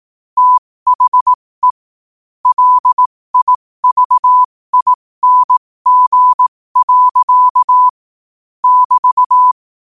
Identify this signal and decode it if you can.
Morse code, heard as a keyed tone, about 18 wpm, spaced as for 15 wpm, 1000 Hz. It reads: THE LIVING. <BT>
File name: qotdfiles / 15wpm